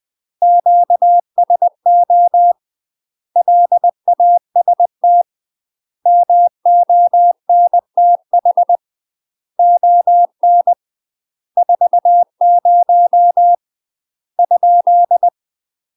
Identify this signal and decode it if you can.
Morse code, heard as a keyed tone, about 20 wpm, 700 Hz. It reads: QSO LAST MONTH ON 40 ?
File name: UK MASTER QSOs TRAINING / 20WPM / UK MASTER QSOs 0069